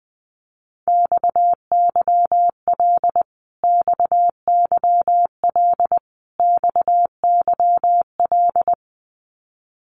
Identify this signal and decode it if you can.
Morse code, heard as a keyed tone, about 20 wpm, 700 Hz. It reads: XYL XYL XYL
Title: Abbreviations Repeated 3x - 20wpm